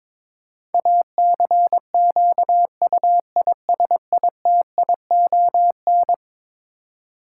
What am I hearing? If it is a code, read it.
Morse code, heard as a keyed tone, about 22 wpm, 700 Hz. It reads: ACQUISITION
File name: practice / 22wpm / Eleven Letter Words - 22wpm